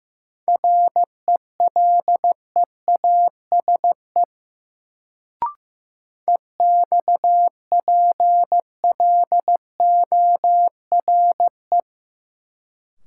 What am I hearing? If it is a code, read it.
Morse code, heard as a keyed tone, about 15 wpm, 700 Hz. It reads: RELEASE EXPLORE
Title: Sentences from Top 1100 Words - New Word Drill Alt - 15wpm